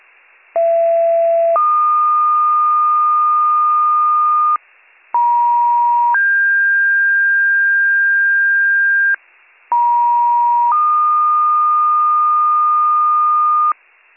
Two-tone page, heard as a one-second tone, then a three-second tone, about 670 then 1220 Hz, about 950 then 1600 Hz, about 950 then 1220 Hz, with noise. It is heard from 0.6 to 13.7 s.